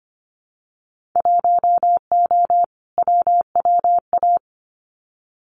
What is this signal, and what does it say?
Morse code, heard as a keyed tone, about 25 wpm, 690 Hz. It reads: 1O WWA
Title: Winter Field Day Exchanges 25wpm